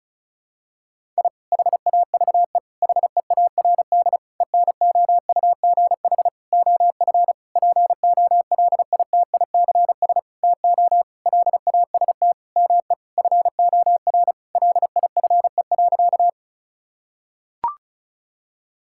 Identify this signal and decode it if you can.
Morse code, heard as a keyed tone, about 36 wpm, 700 Hz. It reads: I HAVE HEARD ENOUGH OF POLITICS TO LAST ME FOR LIFE.
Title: Sentences from Top 1000 Words - Review Part 6 - 35wpm